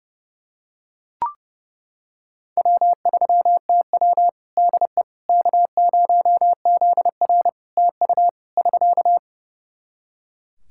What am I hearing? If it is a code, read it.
Morse code, heard as a keyed tone, about 28 wpm, 700 Hz. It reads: W3TW DE K0ZR TU <SK>
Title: K1USN SST QSO Practice 30wpm